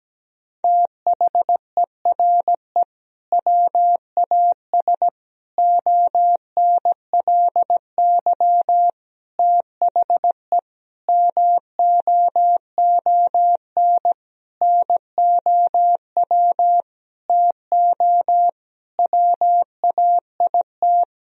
Morse code, heard as a keyed tone, about 17 wpm, 700 Hz. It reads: THERE WAS ONLY THE MOON NOW TO WAIT
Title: Sentences from Top 1000 Words - Review Part 3 - 17wpm